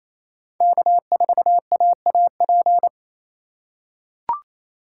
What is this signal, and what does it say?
Morse code, heard as a keyed tone, about 28 wpm, 700 Hz. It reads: K4AAP
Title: USA Call Signs - 4x Character Spacing - 28wpm